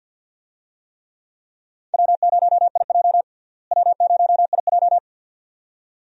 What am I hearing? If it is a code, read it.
Morse code, heard as a keyed tone, about 53 wpm, 690 Hz. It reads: W0IJ W0IJ
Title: USA Call Signs - 4 Characters - Repeated 2x - 50wpm